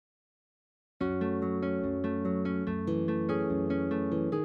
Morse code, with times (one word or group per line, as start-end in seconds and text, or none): none